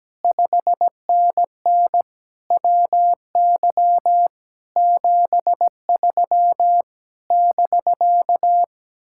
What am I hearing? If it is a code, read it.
Morse code, heard as a keyed tone, about 17 wpm, 700 Hz. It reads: 5NN WY 73 <BK>
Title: Speed-Racing Common QSO Phrases - 17wpm